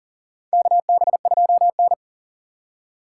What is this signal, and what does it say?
Morse code, heard as a keyed tone, about 37 wpm, 700 Hz. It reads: KB2N
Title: USA Call Signs - 4 Characters - 2x Character Spacing - 40wpm